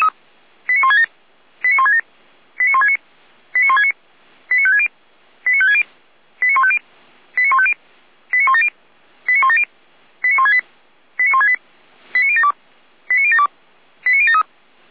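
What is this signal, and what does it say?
Five-tone selective call (ZVEI), heard as quick runs of five tones, tones of about 70 ms. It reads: 89167 89167 89169 89169 89560 89560 89150 89150 89170 89170 89168 89168 89072 89072 89073
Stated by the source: ZVEI five-tone call received by SDR, from feuerwehr-orlamuende.org fire brigade server.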